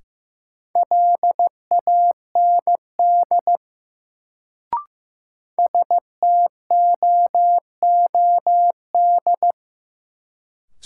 Morse code, heard as a keyed tone, about 15 wpm, 700 Hz. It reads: LAND STOOD
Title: Sentences from Top 500 Words - New Word Drill - 15wpm